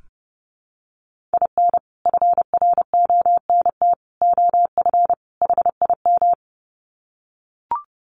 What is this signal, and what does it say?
Morse code, heard as a keyed tone, about 30 wpm, 700 Hz. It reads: IN FRONT OF HIM
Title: Sets of 4 Words 30wpm